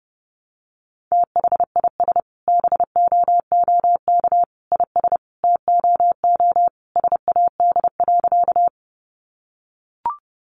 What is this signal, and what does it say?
Morse code, heard as a keyed tone, about 30 wpm, 700 Hz. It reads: THIS BOOK IS TOO SAD.
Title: Additional Easy Phrases 30wpm